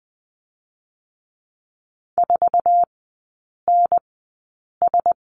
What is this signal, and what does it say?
Morse code, heard as a keyed tone, about 20 wpm, spaced as for 8 wpm, 700 Hz. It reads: 4NS
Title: 060 Lesson 11e - Letter h - Three Characters 20wpm